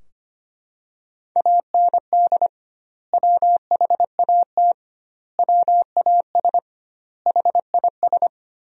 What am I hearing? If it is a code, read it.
Morse code, heard as a keyed tone, about 25 wpm, 700 Hz. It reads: AND WHAT WAS HIS
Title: 2x Word-Spacing - Sentences from Top 200 Words - 25wpm